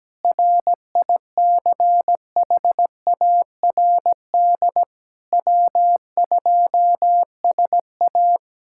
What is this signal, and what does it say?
Morse code, heard as a keyed tone, about 17 wpm, 700 Hz. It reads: RICHARD W2SA
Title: K1USN SST QSO Practice - 17wpm